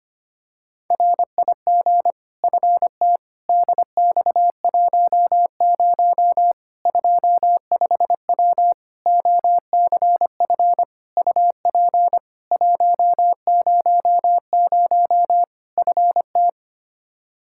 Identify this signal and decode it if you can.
Morse code, heard as a keyed tone, about 25 wpm, 700 Hz. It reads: RIG FT DX10 25W OCF UP 100 FT
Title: Practice QSO - Exchange 2 - Info - 25wpm